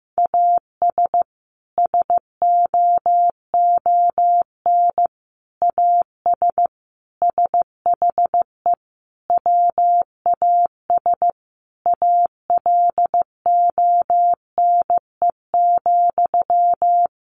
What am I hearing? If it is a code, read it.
Morse code, heard as a keyed tone, about 15 wpm, 700 Hz. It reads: AS SOON AS SHE WAS ALONE,
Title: Sentences from Top 1100 Words 15wpm